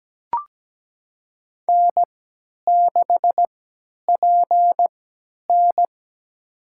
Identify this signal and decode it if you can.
Morse code, heard as a keyed tone, about 17 wpm, spaced as for 10 wpm, 700 Hz. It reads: N6PN
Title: USA Call Signs - 4 Characters - 3x Character Spacing - 17wpm